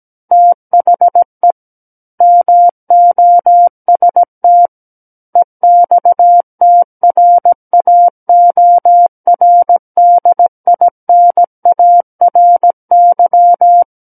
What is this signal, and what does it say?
Morse code, heard as a keyed tone, about 17 wpm, 700 Hz. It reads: THE MOST EXTRAORDINARY
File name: Book One / 08-Chapter 8 - 17 WPM